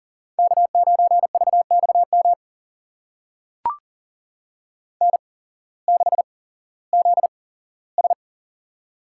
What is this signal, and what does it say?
Morse code, heard as a keyed tone, about 41 wpm, spaced as for 19 wpm, 700 Hz. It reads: K9VXM N6ZS
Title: USA Call Signs - 8x Character Spacing - 40wpm